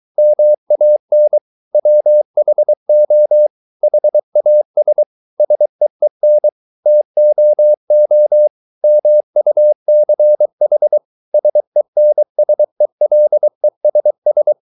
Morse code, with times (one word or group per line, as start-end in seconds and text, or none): 0.2-1.4 MAN
1.7-3.5 WHO
3.8-5.0 HAS
5.4-6.5 SEEN
6.9-8.5 TOO
8.8-11.0 MUCH
11.3-14.5 SENSELESS